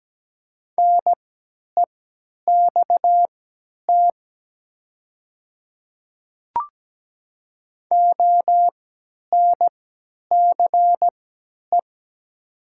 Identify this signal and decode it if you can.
Morse code, heard as a keyed tone, about 17 wpm, spaced as for 10 wpm, 700 Hz. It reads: NEXT ONCE
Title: 3x Character Spacing - Top 500 Words - 17wpm